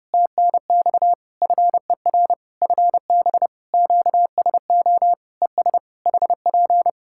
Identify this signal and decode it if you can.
Morse code, heard as a keyed tone, about 30 wpm, 700 Hz. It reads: TNX FER FB QSO ES HP
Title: K1USN SST QSO Practice 30wpm